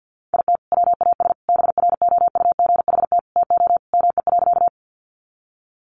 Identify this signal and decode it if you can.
Morse code, heard as a keyed tone, about 50 wpm, 700 Hz. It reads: IT WAS BROUGHT TO ME.
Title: Sentences from Top 500 Words 50wpm